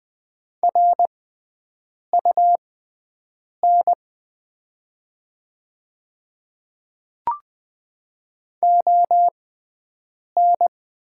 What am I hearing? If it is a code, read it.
Morse code, heard as a keyed tone, about 20 wpm, spaced as for 7 wpm, 700 Hz. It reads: RUN ON
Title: 6x Character Spacing - Top 300 Words - 20wpm